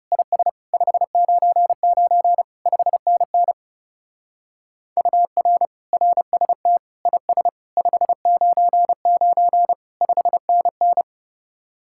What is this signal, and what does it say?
Morse code, heard as a keyed tone, about 29 wpm, 700 Hz. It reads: IS 599 5NN UR RST IS 599 5NN
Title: Speed-Racing Common QSO Phrases - 30wpm